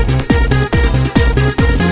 There is no code.